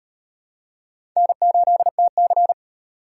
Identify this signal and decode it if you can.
Morse code, heard as a keyed tone, about 38 wpm, 700 Hz. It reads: N8TC